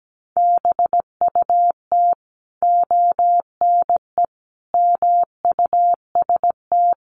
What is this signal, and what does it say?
Morse code, heard as a keyed tone, about 17 wpm, 700 Hz. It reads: BUT ONE MUST
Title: Sentences from Top 1100 Words 17wpm